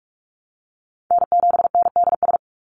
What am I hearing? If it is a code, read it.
Morse code, heard as a keyed tone, about 44 wpm, 700 Hz. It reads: N7NDS
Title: USA Call Signs 45wpm